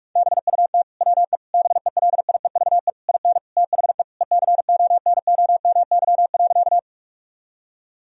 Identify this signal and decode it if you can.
Morse code, heard as a keyed tone, about 42 wpm, 700 Hz. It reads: BUT WE BELIEVE IN THE ECONOMY.
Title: Sentences from Top 1000 Words - Review Part 3 - 45wpm